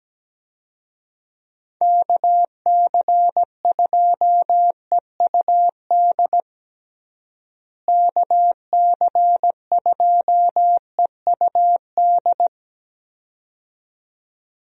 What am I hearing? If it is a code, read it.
Morse code, heard as a keyed tone, about 17 wpm, 700 Hz. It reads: KC2EUD KC2EUD
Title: USA Call Signs - 6 Characters - Repeated 2x - 17wpm